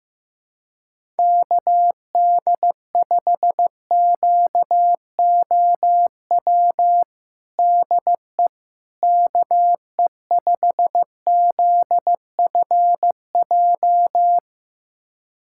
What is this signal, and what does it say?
Morse code, heard as a keyed tone, about 15 wpm, 700 Hz. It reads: KD5QOW DE KE5ZFJ
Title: USA Callsign DE Callsign 15wpm